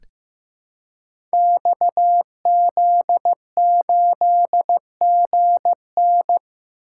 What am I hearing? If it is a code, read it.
Morse code, heard as a keyed tone, about 15 wpm, 700 Hz. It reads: XZ8GN
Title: Code Groups - 5 Letter:Numbers - 15wpm